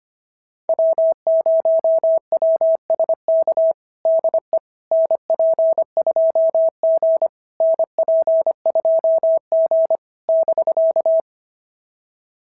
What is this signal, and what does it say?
Morse code, heard as a keyed tone, about 25 wpm, 690 Hz. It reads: W0WSK DE NP2G NP2G <BK>
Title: SOTA QSO Practice 25wpm